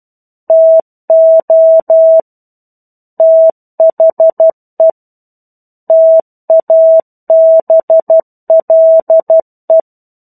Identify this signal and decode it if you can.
Morse code, heard as a keyed tone, about 12 wpm, 650 Hz. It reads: TO THE TABLE
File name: 01-Chapter-01 12 WPM